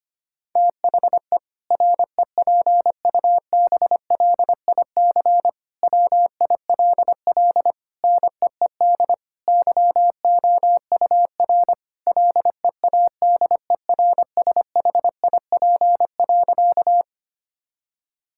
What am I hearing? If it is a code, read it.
Morse code, heard as a keyed tone, about 25 wpm, 700 Hz. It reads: THE REPUBLIC WILL NEED YOUR LEADERSHIP.
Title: Sentences from Top 1600 Words 25wpm